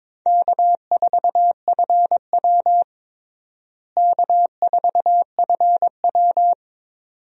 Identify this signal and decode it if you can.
Morse code, heard as a keyed tone, about 22 wpm, 700 Hz. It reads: K4FW K4FW
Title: USA Call Signs - 4 Characters - Repeated 2x - 22wpm